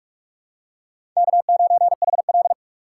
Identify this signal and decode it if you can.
Morse code, heard as a keyed tone, about 44 wpm, 700 Hz. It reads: K9HL
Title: USA Call Signs - 4 Characters - 3x Character Spacing - 45wpm